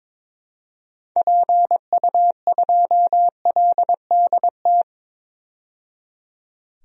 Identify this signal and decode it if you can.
Morse code, heard as a keyed tone, about 22 wpm, 700 Hz. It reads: PU2LDT
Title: DXCC-100 Call Signs - 6 Characters - 22wpm